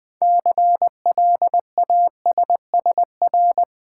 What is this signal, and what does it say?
Morse code, heard as a keyed tone, about 20 wpm, 700 Hz. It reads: CLASSR
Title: Build A Word - 3 Letters At A Time - 9-Letter Words - 20wpm